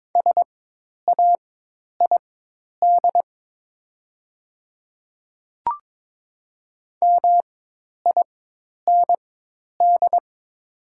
Morse code, heard as a keyed tone, about 22 wpm, spaced as for 10 wpm, 700 Hz. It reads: SAID MIND
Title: 4x Character Spacing - Top 500 Words - 22wpm